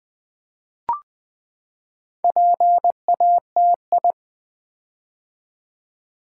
Morse code, heard as a keyed tone, about 20 wpm, 700 Hz. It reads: PATI
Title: Build A Word - 4 Letters At A Time - 8-Letter Words - 20wpm